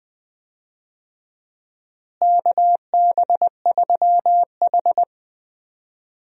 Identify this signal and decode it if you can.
Morse code, heard as a keyed tone, about 20 wpm, 700 Hz. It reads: KB3H
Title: USA Call Signs - 4 Characters - Repeated 3x - 20wpm